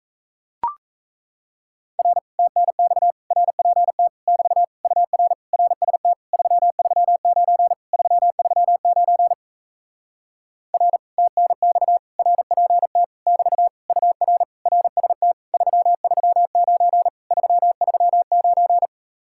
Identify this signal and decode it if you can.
Morse code, heard as a keyed tone, about 37 wpm, 700 Hz. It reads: R TNX RPT <BT> UR RST 339 339 R TNX RPT <BT> UR RST 339 339